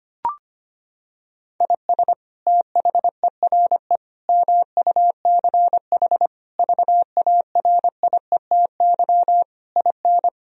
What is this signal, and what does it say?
Morse code, heard as a keyed tone, about 25 wpm, 700 Hz. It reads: IS THERE MUCH VARIETY IN
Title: Sentences from Top 1000 Words - Review Part 4 - 25wpm